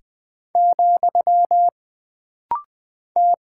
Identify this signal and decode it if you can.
Morse code, heard as a keyed tone, about 20 wpm, 700 Hz. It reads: , T
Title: Single Character (Letter:Number:Punctuation:Prosign) - Rapid-Fire - 20wpm